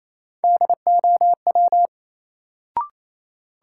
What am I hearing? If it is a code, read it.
Morse code, heard as a keyed tone, about 28 wpm, 700 Hz. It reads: DOW
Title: Sets of 5 Words - New Word Drill - 28wpm